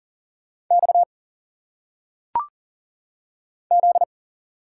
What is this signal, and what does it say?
Morse code, heard as a keyed tone, about 39 wpm, 700 Hz. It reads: X Z
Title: Single Letters - Phonetic Pronunciation 40wpm